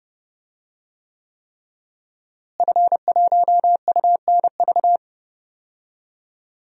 Morse code, heard as a keyed tone, about 29 wpm, 700 Hz. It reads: F1UNV
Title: DXCC-100 Call Signs - 5 Characters - Repeated 3x - 30wpm